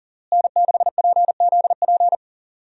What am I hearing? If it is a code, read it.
Morse code, heard as a keyed tone, about 40 wpm, 700 Hz. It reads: N6PZP
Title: USA Call Signs - 5 Characters - Repeated 2x - 40wpm